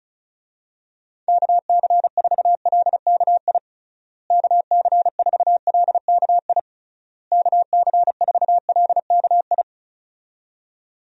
Morse code, heard as a keyed tone, about 34 wpm, 700 Hz. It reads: KC4LKI KC4LKI KC4LKI
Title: USA Call Signs - 6 Characters - Repeated 3x - 35wpm